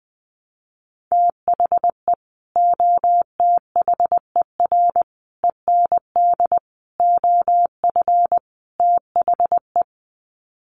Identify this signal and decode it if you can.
Morse code, heard as a keyed tone, about 20 wpm, 700 Hz. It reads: THE OTHER END OF THE
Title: Sets of 5 Words 20wpm